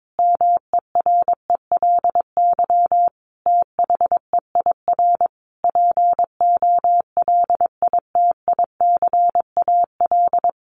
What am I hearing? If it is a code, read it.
Morse code, heard as a keyed tone, about 22 wpm, 700 Hz. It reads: MERELY THEIR POLITICAL
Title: Sentences from Top 1400 Words 22wpm